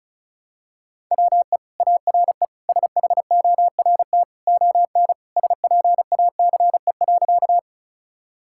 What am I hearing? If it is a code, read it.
Morse code, heard as a keyed tone, about 36 wpm, 690 Hz. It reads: WE ARE SHORT ON SPACE.
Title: Sentences from Top 1000 Words - Review Part 5 - 35wpm